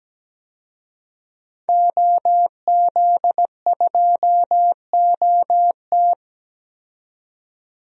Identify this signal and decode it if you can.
Morse code, heard as a keyed tone, about 17 wpm, 700 Hz. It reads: OZ2OT
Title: DXCC-100 Call Signs - 5 Characters - 17wpm